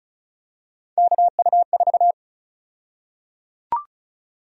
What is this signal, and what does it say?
Morse code, heard as a keyed tone, about 36 wpm, 700 Hz. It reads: KU4
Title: USA Call Sign Prefixes - 35wpm